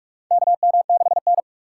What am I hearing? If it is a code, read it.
Morse code, heard as a keyed tone, about 44 wpm, 700 Hz. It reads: KM6N
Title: USA Call Signs - 4 Characters - 2x Character Spacing - 45wpm